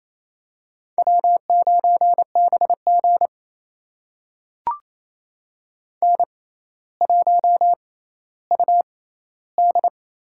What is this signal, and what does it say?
Morse code, heard as a keyed tone, about 27 wpm, spaced as for 14 wpm, 700 Hz. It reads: W9BG N1UD
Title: USA Call Signs - 4 Characters - 6x Character Spacing - 28wpm